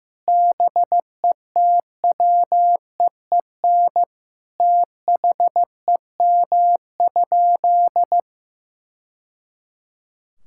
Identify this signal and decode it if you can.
Morse code, heard as a keyed tone, about 15 wpm, 700 Hz. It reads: BETWEEN THEM?
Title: Sentences from Top 1000 Words - Review Part 5 - 15wpm